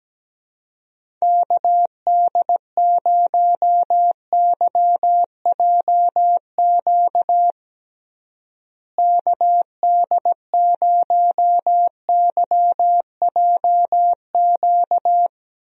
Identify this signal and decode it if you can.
Morse code, heard as a keyed tone, about 17 wpm, 700 Hz. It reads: KD0YJQ KD0YJQ
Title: USA Call Signs - 6 Characters - Repeated 3x - 17wpm